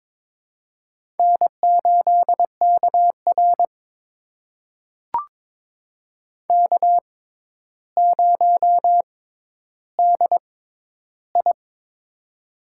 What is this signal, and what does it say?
Morse code, heard as a keyed tone, about 22 wpm, spaced as for 11 wpm, 700 Hz. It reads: N8KR K0DI
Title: USA Call Signs - 4 Characters - 6x Character Spacing - 22wpm